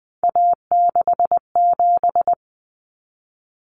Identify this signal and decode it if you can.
Morse code, heard as a keyed tone, about 20 wpm, 700 Hz. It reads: A67
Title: Common Cut Numbers - 3-Digit 20wpm